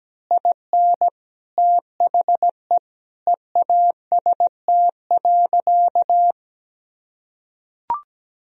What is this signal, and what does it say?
Morse code, heard as a keyed tone, about 17 wpm, 700 Hz. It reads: IN THE EAST.
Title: Sentences from Top 1000 Words - Review Part 3 - 17wpm